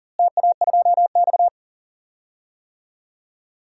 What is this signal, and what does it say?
Morse code, heard as a keyed tone, about 40 wpm, 700 Hz. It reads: TA2X
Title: DXCC-100 Call Signs - 4 Characters - Repeated 3x - 40wpm